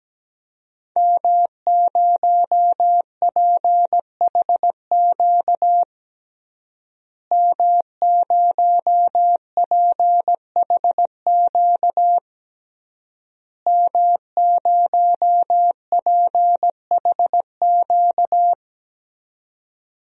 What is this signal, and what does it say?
Morse code, heard as a keyed tone, about 17 wpm, 700 Hz. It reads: M0PHQ M0PHQ M0PHQ
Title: DXCC-100 Call Signs - 5 Characters - Repeated 3x - 17wpm